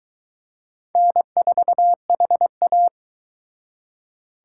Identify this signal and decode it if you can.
Morse code, heard as a keyed tone, about 23 wpm, 700 Hz. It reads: N4HA